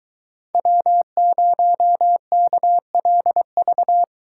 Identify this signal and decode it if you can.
Morse code, heard as a keyed tone, about 22 wpm, 700 Hz. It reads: W0KLV